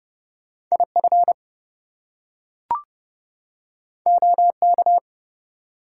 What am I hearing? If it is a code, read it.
Morse code, heard as a keyed tone, about 31 wpm, 700 Hz. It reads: IF OK
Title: Two Letter Words - 30wpm